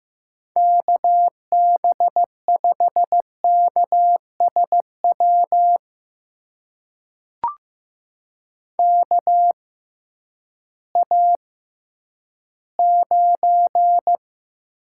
Morse code, heard as a keyed tone, about 15 wpm, spaced as for 10 wpm, 700 Hz. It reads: KB5KSW KA9
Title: USA Call Signs - 6 Characters - 6x Character Spacing - 15wpm